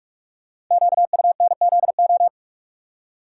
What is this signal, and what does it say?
Morse code, heard as a keyed tone, about 44 wpm, 690 Hz. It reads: QUNZO
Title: Code Groups - 5 Letters - 45wpm